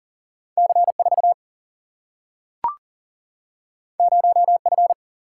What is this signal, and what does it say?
Morse code, heard as a keyed tone, about 40 wpm, 700 Hz. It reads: C 4 0 F
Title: Single-Letter Single-Number 40wpm